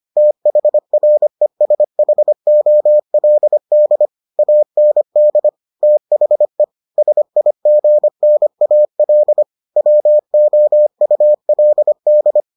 Morse code, heard as a keyed tone, about 25 wpm, 600 Hz. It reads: THRESHOLD AND THE SIGNAL WOULD